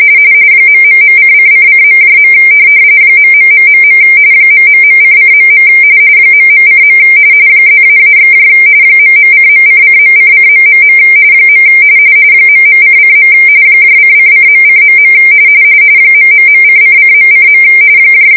RTTY, about 45.45 baud. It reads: ...::::.. ..... ...::.. ..:. +--/ ..:.::.::... ..::. :::.:.. ...:::.:.